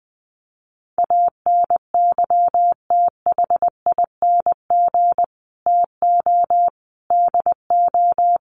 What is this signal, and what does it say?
Morse code, heard as a keyed tone, about 20 wpm, 700 Hz. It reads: ANYTHING TO DO
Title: Sets of 5 Words Encore 20wpm